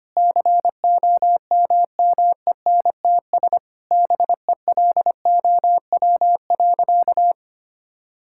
Morse code, heard as a keyed tone, about 25 wpm, 700 Hz. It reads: COMMENTS BELOW.
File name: practice / 25wpm / Sentences with 9-Word Phrases - 25wpm